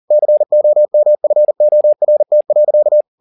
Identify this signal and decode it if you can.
Morse code, heard as a keyed tone, about 40 wpm, 600 Hz. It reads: COMFORT.